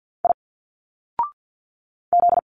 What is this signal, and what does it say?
Morse code, heard as a keyed tone, about 51 wpm, 700 Hz. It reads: I Z
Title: Single Letters 50wpm